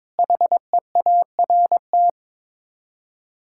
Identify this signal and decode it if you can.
Morse code, heard as a keyed tone, about 22 wpm, 700 Hz. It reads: HEART
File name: practice / 22wpm / Sets of 6 Words - 22wpm